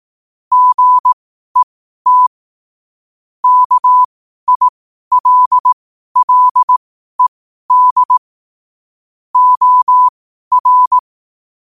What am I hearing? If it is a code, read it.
Morse code, heard as a keyed tone, about 18 wpm, spaced as for 12 wpm, 1000 Hz. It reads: GET KILLED OR
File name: qotdfiles / 12wpm